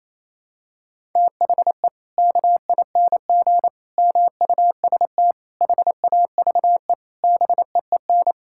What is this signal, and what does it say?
Morse code, heard as a keyed tone, about 28 wpm, 700 Hz. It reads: THE KING MUST HAVE BEEN
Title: Sentences from Top 1000 Words - Review Part 5 - 28wpm